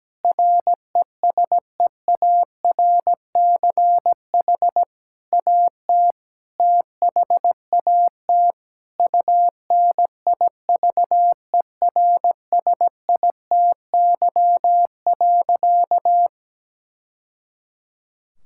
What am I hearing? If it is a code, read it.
Morse code, heard as a keyed tone, about 17 wpm, 700 Hz. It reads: RESEARCH AT THAT UNIVERSITY.
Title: Sentences from Top 1000 Words - Review Part 6 - 17wpm